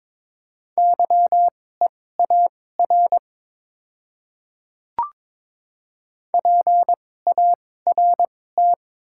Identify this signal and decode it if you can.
Morse code, heard as a keyed tone, about 22 wpm, spaced as for 16 wpm, 700 Hz. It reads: YEAR PART
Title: 2x Character Spacing - Top 200 Words - 22wpm